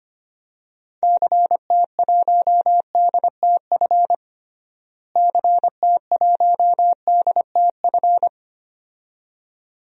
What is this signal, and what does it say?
Morse code, heard as a keyed tone, about 25 wpm, 700 Hz. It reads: CT1DTF CT1DTF
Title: DXCC-100 Call Signs - 6 Characters - Repeated 2x - 25wpm